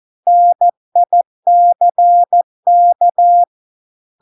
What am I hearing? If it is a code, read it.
Morse code, heard as a keyed tone, about 14 wpm, 700 Hz. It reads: NICK